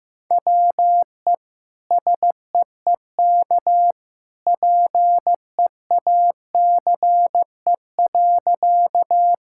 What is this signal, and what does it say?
Morse code, heard as a keyed tone, about 16 wpm, 700 Hz. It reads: WE SEEK PEACE.
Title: Sentences from Top 1600 Words 15wpm